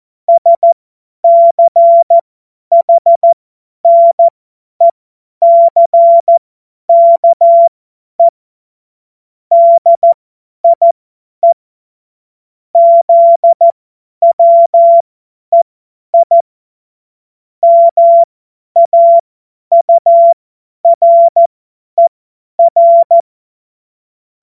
Morse code, heard as a keyed tone, about 14 wpm, spaced as for 10 wpm, 680 Hz. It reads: SCHNECKE DIE ZWEI MAURER
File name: Morsekurs / Witze / Witz 31.Oktober 2023 50 F70